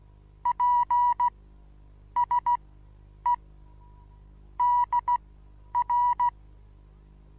Morse code, heard as a keyed tone, about 17 wpm, spaced as for 9 wpm, 970 Hz. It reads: PSE DR
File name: Onlinelehrgang / morsen / DJ4UF Morsen25